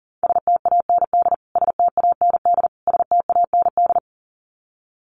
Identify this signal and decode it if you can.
Morse code, heard as a keyed tone, about 37 wpm, 700 Hz. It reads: STAND STAND STAND
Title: Top 500 Words Repeated 40wpm